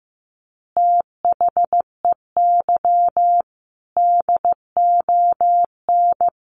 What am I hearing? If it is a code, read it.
Morse code, heard as a keyed tone, about 15 wpm, 700 Hz. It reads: THEY DON
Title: Easy Phrases 15wpm